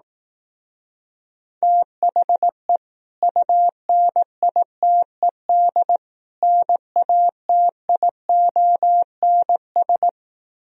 Morse code, heard as a keyed tone, about 18 wpm, 700 Hz. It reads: THE UNITED NATIONS
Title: Speed-Racing - Sets of 3 Words 15wpm